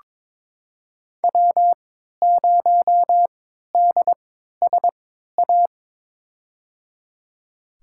Morse code, heard as a keyed tone, about 23 wpm, spaced as for 13 wpm, 700 Hz. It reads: W0DSA
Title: USA Call Signs - 5 Characters - 3x Character Spacing - 22wpm